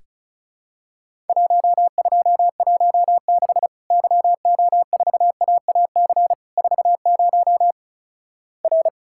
Morse code, heard as a keyed tone, about 36 wpm, 700 Hz. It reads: 1216 YO4AAC 40 R
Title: Worked All Europe DX Contest - QTC Exchange - 35wpm